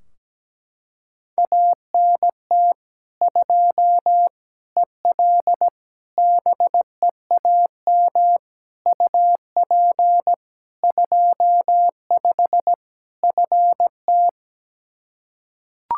Morse code, heard as a keyed tone, about 17 wpm, 700 Hz. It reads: ANT 2 EL BEAM UP 25 FT
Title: K1USN SST QSO Practice 17wpm